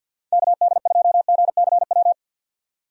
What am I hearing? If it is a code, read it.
Morse code, heard as a keyed tone, about 50 wpm, 700 Hz. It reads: KD2GCW
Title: USA Call Signs - 6 Characters - 50wpm